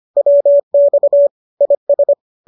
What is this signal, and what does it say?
Morse code, heard as a keyed tone, about 25 wpm, 570 Hz. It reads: WX IS
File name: qso221 25